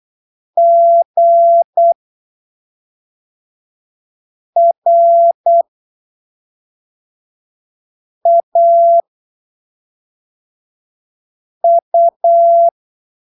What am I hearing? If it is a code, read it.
Morse code, heard as a keyed tone, about 8 wpm, spaced as for 3 wpm, 680 Hz. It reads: GRAU
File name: arquivos / texto / texto 11 08ppm